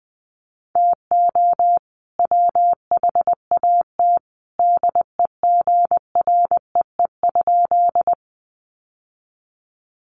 Morse code, heard as a keyed tone, about 20 wpm, 700 Hz. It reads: TO WHAT DEGREE?
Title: Sentences from Top 1500 Words 20wpm